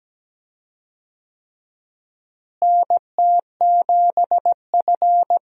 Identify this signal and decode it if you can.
Morse code, heard as a keyed tone, about 17 wpm, 700 Hz. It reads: NT7F
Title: USA Call Signs - 4 Characters - Repeated 3x - 17wpm